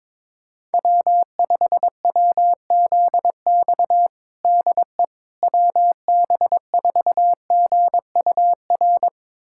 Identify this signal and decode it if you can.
Morse code, heard as a keyed tone, about 22 wpm, 700 Hz. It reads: W5WZX DE WB4GUR
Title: USA Callsign DE Callsign 22wpm